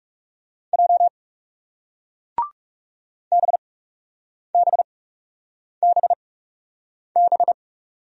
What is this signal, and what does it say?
Morse code, heard as a keyed tone, about 38 wpm, spaced as for 8 wpm, 700 Hz. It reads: J BBBB